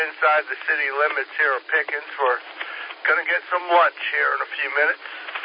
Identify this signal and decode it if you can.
Emergency Alert System (SAME) tones, heard as harsh buzzing bursts, with speech. It is heard from 1.1 to 5.2 s.